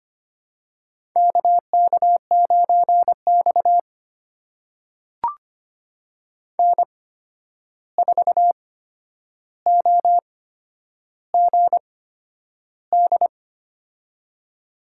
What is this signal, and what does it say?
Morse code, heard as a keyed tone, about 25 wpm, spaced as for 10 wpm, 700 Hz. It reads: KK9X N4OGD
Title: USA Call Signs - 8x Character Spacing - 25wpm